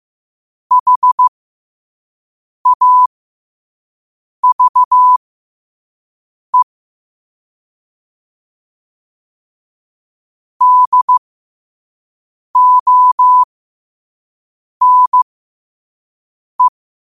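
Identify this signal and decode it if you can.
Morse code, heard as a keyed tone, about 15 wpm, spaced as for 5 wpm, 1000 Hz. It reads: HAVE DONE
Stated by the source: 5 wpm Morse code practice recording.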